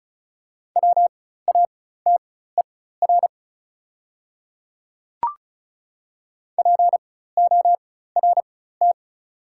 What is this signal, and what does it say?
Morse code, heard as a keyed tone, about 36 wpm, spaced as for 16 wpm, 690 Hz. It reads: WATER PORT
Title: 4x Character Spacing - Top 200 Words - 35wpm